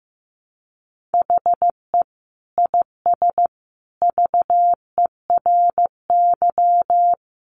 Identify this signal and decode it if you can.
Morse code, heard as a keyed tone, about 15 wpm, 700 Hz. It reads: HE IS VERY
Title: Sentences from Top 1400 Words 15wpm